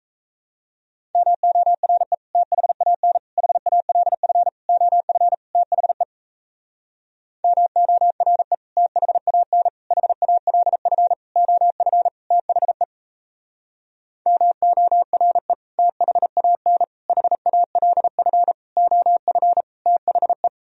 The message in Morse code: MORE THAN HALF OF THE MORE THAN HALF OF THE MORE THAN HALF OF THE

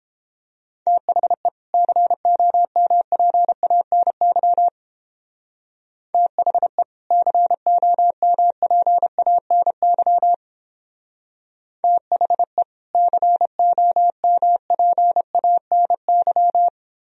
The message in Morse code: THE COMPANY THE COMPANY THE COMPANY

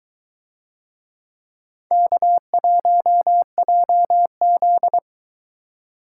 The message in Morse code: K1JZ

